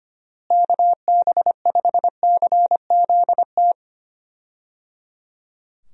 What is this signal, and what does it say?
Morse code, heard as a keyed tone, about 25 wpm, 700 Hz. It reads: KB5CZT
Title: USA Call Signs - 6 Characters - Repeated 3x - 25wpm